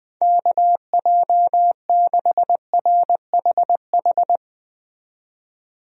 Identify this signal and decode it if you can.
Morse code, heard as a keyed tone, about 20 wpm, 700 Hz. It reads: KJ6RHH
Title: USA Call Signs - 6 Characters - 20wpm